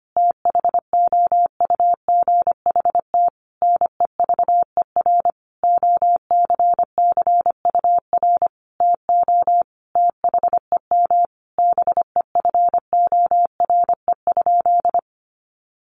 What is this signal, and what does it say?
Morse code, heard as a keyed tone, about 25 wpm, 700 Hz. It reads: THOUGHT NEVER OCCUR TO THEM BEFORE?
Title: Sentences from Top 900 Words 25wpm